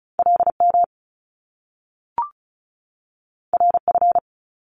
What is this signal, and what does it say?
Morse code, heard as a keyed tone, about 36 wpm, 700 Hz. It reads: LM RF
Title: Top Two Letter Combinations in Words 35wpm